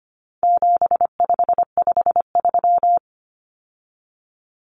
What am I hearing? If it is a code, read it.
Morse code, heard as a keyed tone, about 25 wpm, 700 Hz. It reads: 7553
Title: Numbers - 4-Digit 25wpm